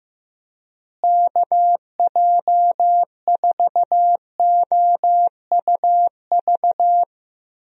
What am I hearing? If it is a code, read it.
Morse code, heard as a keyed tone, about 16 wpm, 700 Hz. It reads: KJ4OUV
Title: USA Call Signs - 6 Characters - Repeated 3x - 15wpm